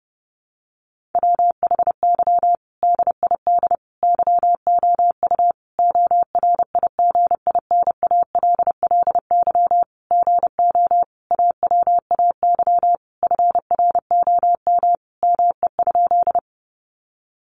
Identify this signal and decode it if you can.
Morse code, heard as a keyed tone, about 30 wpm, 700 Hz. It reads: WHY DID YOU ORIGINALLY GO AWAY FROM ME?
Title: Sentences from Top 1500 Words 30wpm